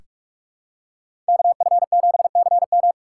Morse code, heard as a keyed tone, about 46 wpm, 700 Hz. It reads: KF7CM
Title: USA Call Signs - 5 Characters - 4x Character Spacing - 45wpm